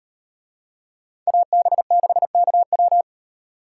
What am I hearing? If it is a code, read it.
Morse code, heard as a keyed tone, about 36 wpm, 700 Hz. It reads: AB6KW